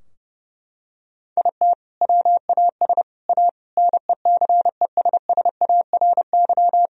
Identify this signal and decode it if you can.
Morse code, heard as a keyed tone, about 30 wpm, 700 Hz. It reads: IT WAS A NECESSARY
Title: Sentences from Top 1000 Words - Review Part 2 - 30wpm